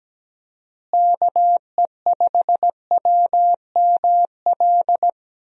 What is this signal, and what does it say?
Morse code, heard as a keyed tone, about 17 wpm, 700 Hz. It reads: KE5WML
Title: USA Call Signs - 6 Characters - Repeated 2x - 17wpm